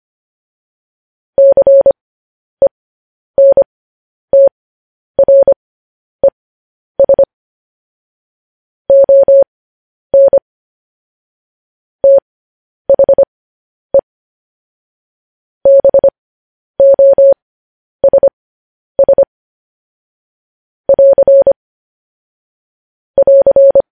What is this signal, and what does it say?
Morse code, heard as a keyed tone, about 25 wpm, spaced as for 10 wpm, 550 Hz. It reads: CENTRES ON THE BOSS <AR> <AR>